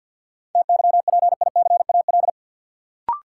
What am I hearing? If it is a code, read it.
Morse code, heard as a keyed tone, about 53 wpm, 700 Hz. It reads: TYPICAL
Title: Sentences from Top 1100 Words - New Word Drill Alt - 50wpm